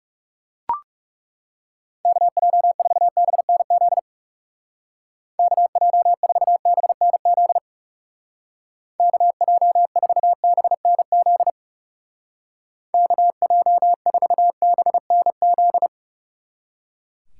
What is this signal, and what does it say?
Morse code, heard as a keyed tone, about 36 wpm, 700 Hz. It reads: KJ4BNZ KJ4BNZ KJ4BNZ KJ4BNZ
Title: Speed Racing - USA 2x3 Call Signs 30wpm